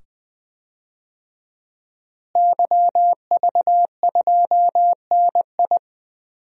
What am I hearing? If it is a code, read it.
Morse code, heard as a keyed tone, about 20 wpm, 700 Hz. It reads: YV2NI
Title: DXCC-100 Call Signs - 5 Characters - Repeated 2x - 20wpm